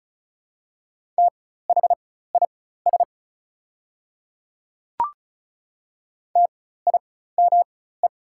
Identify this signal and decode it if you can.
Morse code, heard as a keyed tone, about 33 wpm, spaced as for 16 wpm, 690 Hz. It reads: THIS TIME
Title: 4x Character Spacing - Top 100 Words - 35wpm